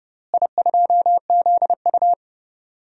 Morse code, heard as a keyed tone, about 30 wpm, 700 Hz. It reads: I2ZU
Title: DXCC-100 Call Signs - 4 Characters - Repeated 3x - 30wpm